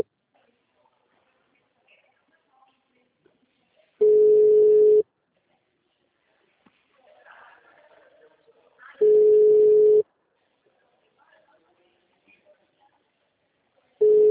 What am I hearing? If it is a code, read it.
Ringback tone (Europe, 425 Hz).